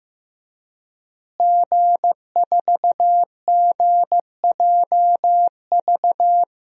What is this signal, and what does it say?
Morse code, heard as a keyed tone, about 15 wpm, 700 Hz. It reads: G4GJV
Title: DXCC-100 Call Signs - 5 Characters - Repeated 3x - 15wpm